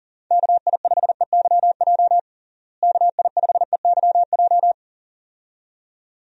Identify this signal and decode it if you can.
Morse code, heard as a keyed tone, about 40 wpm, 700 Hz. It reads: KI5EYJ KI5EYJ
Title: USA Call Signs - 6 Characters - Repeated 2x - 40wpm